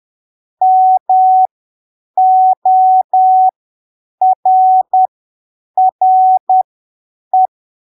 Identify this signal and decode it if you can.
Morse code, heard as a keyed tone, about 10 wpm, spaced as for 7 wpm, 750 Hz. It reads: MORRE
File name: arquivos / texto / texto 20 10ppm